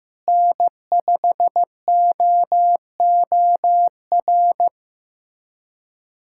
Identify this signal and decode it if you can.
Morse code, heard as a keyed tone, about 15 wpm, 700 Hz. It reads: N5OOR
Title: USA Callsign DE Callsign - 5 Characters - 15wpm